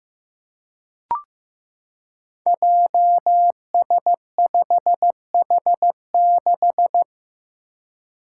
Morse code, heard as a keyed tone, about 15 wpm, 700 Hz. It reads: JS5H6
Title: Five Challenging Letter-Number Combinations 15wpm